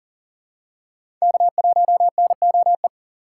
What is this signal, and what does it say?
Morse code, heard as a keyed tone, about 40 wpm, 700 Hz. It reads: K1NOE
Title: USA Call Signs - 5 Characters - 3x Character Spacing - 40wpm